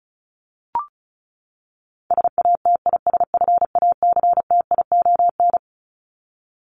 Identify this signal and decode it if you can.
Morse code, heard as a keyed tone, about 34 wpm, 700 Hz. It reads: SATISFACTION
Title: Top 1751-2000 Words 35wpm